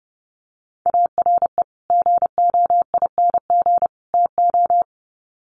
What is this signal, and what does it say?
Morse code, heard as a keyed tone, about 30 wpm, 700 Hz. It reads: ARE GOING TO
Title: Sets of 3 Words 30wpm